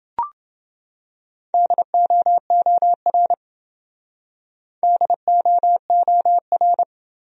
Morse code, heard as a keyed tone, about 28 wpm, 700 Hz. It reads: DOOR DOOR